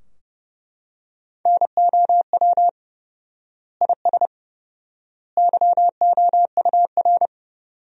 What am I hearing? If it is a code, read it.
Morse code, heard as a keyed tone, about 32 wpm, 700 Hz. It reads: NOW IS YOUR
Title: 4x Word-Spacing - Sentences from Top 100 Words - 30wpm